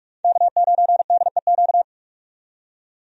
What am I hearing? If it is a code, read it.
Morse code, heard as a keyed tone, about 43 wpm, 700 Hz. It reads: K9DEQ
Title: USA Call Signs - 5 Characters - 45wpm